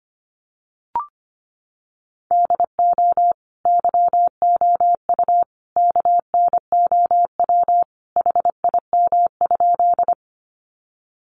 Morse code, heard as a keyed tone, about 25 wpm, 700 Hz. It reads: DO YOU KNOW HIM?
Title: Sentences from Top 100 Words 25wpm